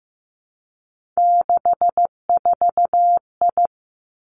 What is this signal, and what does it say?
Morse code, heard as a keyed tone, about 15 wpm, 700 Hz. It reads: 64I
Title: Three Challenging Letter-Number Combinations 15wpm